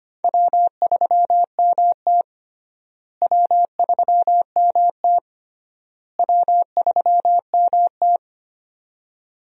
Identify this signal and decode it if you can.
Morse code, heard as a keyed tone, about 25 wpm, 700 Hz. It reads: W3MT W3MT W3MT
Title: USA Call Signs - 4 Characters - Repeated 3x - 25wpm